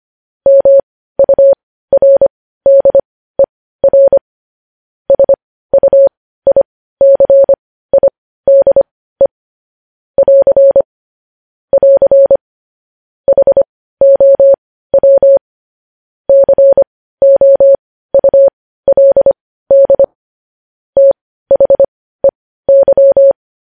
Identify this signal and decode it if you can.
Morse code, heard as a keyed tone, about 25 wpm, spaced as for 15 wpm, 550 Hz. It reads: MURDER SUICIDE <AR> <AR> HOW COULD THEY
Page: news headlines in morse code at 15 wpm